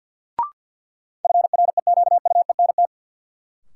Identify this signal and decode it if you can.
Morse code, heard as a keyed tone, about 47 wpm, 700 Hz. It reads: FREQUENT
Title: Sentences from Top 1400 Words - New Word Drill - 50wpm